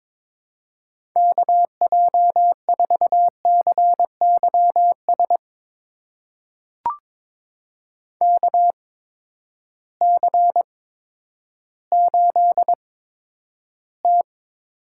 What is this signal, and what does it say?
Morse code, heard as a keyed tone, about 22 wpm, spaced as for 11 wpm, 700 Hz. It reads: KJ4CYS KC8T
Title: USA Call Signs - 6 Characters - 8x Character Spacing - 22wpm